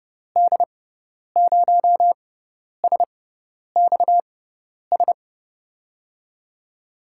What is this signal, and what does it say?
Morse code, heard as a keyed tone, about 30 wpm, spaced as for 10 wpm, 700 Hz. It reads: D0SXS
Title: USA Call Signs - 6 Characters - 6x Character Spacing - 30wpm